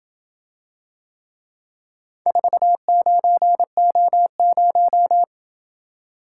Morse code, heard as a keyed tone, about 26 wpm, 700 Hz. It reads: 49O0